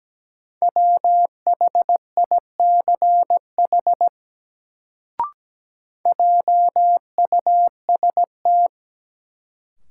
Morse code, heard as a keyed tone, about 17 wpm, 700 Hz. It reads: WHICH JUST
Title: Sets of 2 Words - New Word Drill - 17wpm